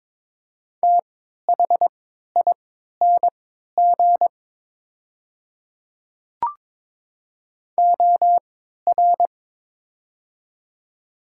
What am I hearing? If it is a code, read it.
Morse code, heard as a keyed tone, about 22 wpm, spaced as for 13 wpm, 700 Hz. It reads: THING OR
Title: 3x Character Spacing - Top 100 Words - 22wpm